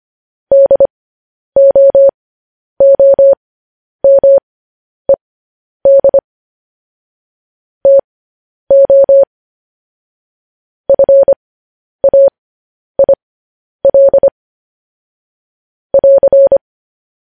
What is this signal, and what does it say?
Morse code, heard as a keyed tone, about 25 wpm, spaced as for 10 wpm, 550 Hz. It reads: DOOMED TO FAIL <AR>
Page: news headlines in morse code at 10 wpm